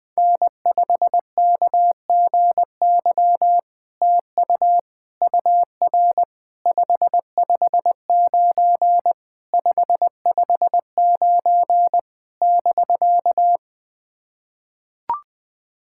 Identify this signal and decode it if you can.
Morse code, heard as a keyed tone, about 20 wpm, 700 Hz. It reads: N5KGY TU UR 559 559 <BK>
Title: POTA QSO Practice - 20wpm